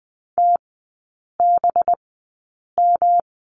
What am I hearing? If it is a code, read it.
Morse code, heard as a keyed tone, about 20 wpm, spaced as for 8 wpm, 700 Hz. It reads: TBM
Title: 205 Lesson 33e - Letter j - Three Characters 20wpm